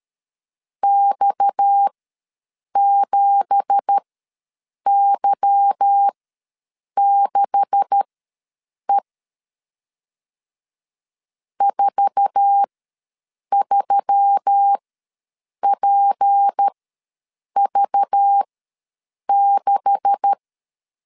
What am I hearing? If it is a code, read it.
Morse code, heard as a keyed tone, about 13 wpm, spaced as for 7 wpm, 800 Hz. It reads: X7Y6E 43PV6
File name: LEZIONI / 9 2867 ws 12wpm 3431